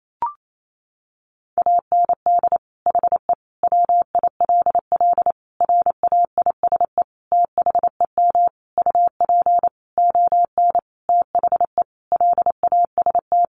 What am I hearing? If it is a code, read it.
Morse code, heard as a keyed tone, about 28 wpm, 700 Hz. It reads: AND HE WILL RAISE THEM UP ON THE LAST
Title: Sentences from Top 1400 Words 28wpm